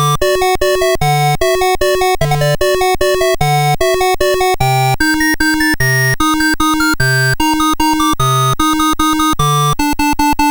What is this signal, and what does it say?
There is no code.